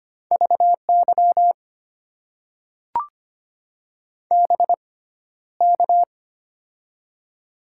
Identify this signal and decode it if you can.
Morse code, heard as a keyed tone, about 25 wpm, spaced as for 13 wpm, 700 Hz. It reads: VY BK
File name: Top 90 QSO Elements - 6x Character Spacing - 25wpm